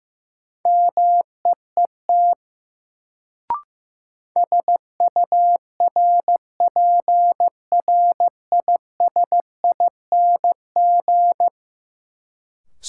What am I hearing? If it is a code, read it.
Morse code, heard as a keyed tone, about 15 wpm, 700 Hz. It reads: MEET SURPRISING
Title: Sets of 5 Words - New Word Drill - 15wpm